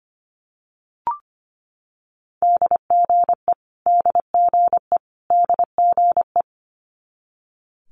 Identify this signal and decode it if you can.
Morse code, heard as a keyed tone, about 25 wpm, 700 Hz. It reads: DGE DGE DGE
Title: Multi-letter Phonemes Repeated 3x 25wpm